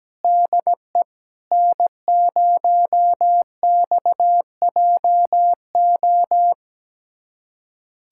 Morse code, heard as a keyed tone, about 17 wpm, 700 Hz. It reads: DE N0XJO
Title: USA Callsign DE Callsign - 5 Characters - 17wpm